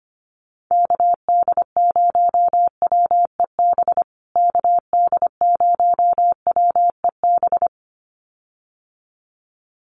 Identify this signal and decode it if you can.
Morse code, heard as a keyed tone, about 25 wpm, 700 Hz. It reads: KD0WEB KD0WEB
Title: USA Call Signs Repeated 2x 25wpm